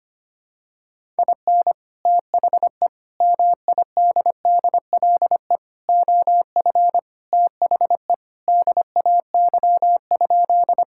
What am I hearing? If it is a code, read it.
Morse code, heard as a keyed tone, about 25 wpm, 700 Hz. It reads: IN THE MIDDLE OF THE DAY?
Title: Sentences from Top 1000 Words - Review Part 6 - 25wpm